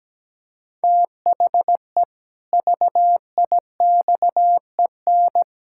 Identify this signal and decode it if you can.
Morse code, heard as a keyed tone, about 17 wpm, 700 Hz. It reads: THE VIXEN
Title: Pangram Challenge v2 - 17wpm